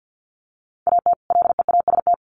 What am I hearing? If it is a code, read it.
Morse code, heard as a keyed tone, about 50 wpm, 700 Hz. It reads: AT LEAST
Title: Sets of 2 Words 50wpm